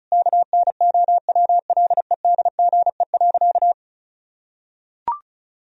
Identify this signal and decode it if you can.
Morse code, heard as a keyed tone, about 36 wpm, 700 Hz. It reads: KNOWLEDGE.
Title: Sentences from Top 1000 Words - Review Part 4 - 35wpm